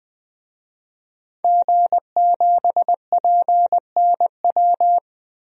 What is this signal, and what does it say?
Morse code, heard as a keyed tone, about 20 wpm, 700 Hz. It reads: G7PNW
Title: DXCC-100 Call Signs - 5 Characters - 20wpm